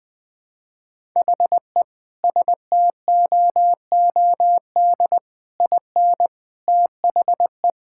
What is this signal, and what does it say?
Morse code, heard as a keyed tone, about 21 wpm, 700 Hz. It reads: HE STOOD IN THE
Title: Sentences from Top 1000 Words - Review Part 6 - 20wpm